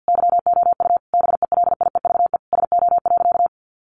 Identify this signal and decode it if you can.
Morse code, heard as a keyed tone, about 50 wpm, 700 Hz. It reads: YOU BELIEVE SO.
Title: Sentences from Top 1000 Words 50wpm